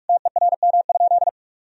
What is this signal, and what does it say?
Morse code, heard as a keyed tone, about 44 wpm, 700 Hz. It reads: TERM?
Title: Sentences from Top 1000 Words - Review Part 4 - 45wpm